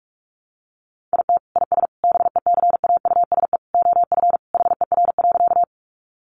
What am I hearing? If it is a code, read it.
Morse code, heard as a keyed tone, about 44 wpm, 700 Hz. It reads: IT IS BECAUSE OF HER.
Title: Sentences from Top 600 Words 45wpm